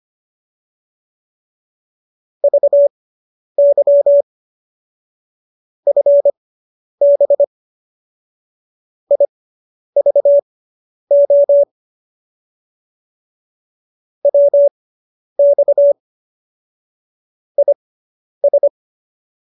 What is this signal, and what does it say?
Morse code, heard as a keyed tone, about 25 wpm, spaced as for 10 wpm, 570 Hz. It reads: VY FB IVO WX IS